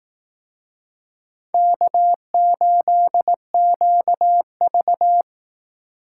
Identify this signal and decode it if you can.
Morse code, heard as a keyed tone, about 18 wpm, 700 Hz. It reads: K8QV
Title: Speed Racing - USA 1x2 Call Signs 15wpm